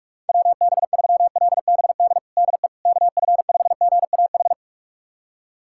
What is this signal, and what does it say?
Morse code, heard as a keyed tone, about 46 wpm, 700 Hz. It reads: WB3LBD DE KF5GAH
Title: USA Callsign DE Callsign 45wpm